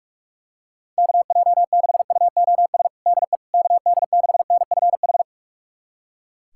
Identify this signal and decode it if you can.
Morse code, heard as a keyed tone, about 41 wpm, 700 Hz. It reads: KJ6UOS DE KD6NFH
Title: USA Callsign DE Callsign - 6 Characters - 45wpm